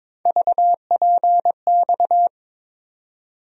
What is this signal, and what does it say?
Morse code, heard as a keyed tone, about 22 wpm, 700 Hz. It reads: VPX
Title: USA Call Sign Postfixes - 3 Letters 22wpm